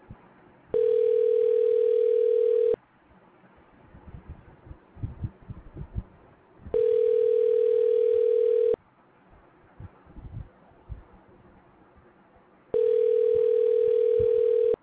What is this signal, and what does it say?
Ringback tone (North America), 440 + 480 Hz.